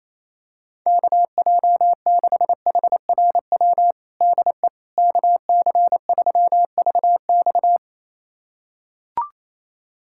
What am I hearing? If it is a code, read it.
Morse code, heard as a keyed tone, about 28 wpm, 700 Hz. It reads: KJ6HRW DE KC3VX
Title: USA Callsign DE Callsign 28wpm